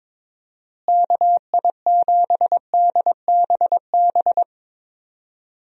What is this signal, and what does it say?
Morse code, heard as a keyed tone, about 21 wpm, 700 Hz. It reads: KI7DBB
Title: USA Call Signs - 6 Characters - Repeated 2x - 22wpm